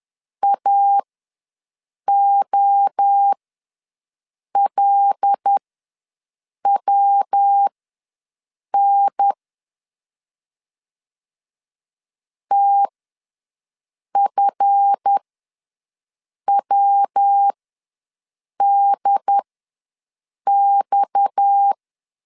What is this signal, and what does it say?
Morse code, heard as a keyed tone, about 10 wpm, spaced as for 6 wpm, 800 Hz. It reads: AOLWN TFWDX
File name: LEZIONI / 7 GF9Z / 7 GF ws 10wpm 3918